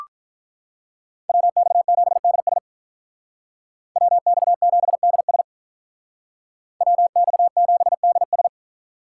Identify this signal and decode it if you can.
Morse code, heard as a keyed tone, about 44 wpm, 700 Hz. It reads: WX7DS WX7DS WX7DS